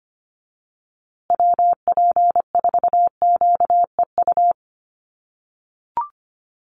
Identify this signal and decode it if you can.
Morse code, heard as a keyed tone, about 25 wpm, 700 Hz. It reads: WP4QEU
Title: USA Call Signs Repeated 3x 25wpm